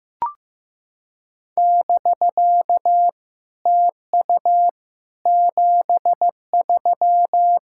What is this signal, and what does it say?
Morse code, heard as a keyed tone, about 15 wpm, 700 Hz. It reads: <BK> TU 73
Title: SOTA QSO Practice 15wpm